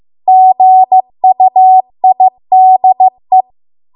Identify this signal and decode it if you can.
Morse code, heard as a keyed tone, about 15 wpm, 750 Hz. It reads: GUIDE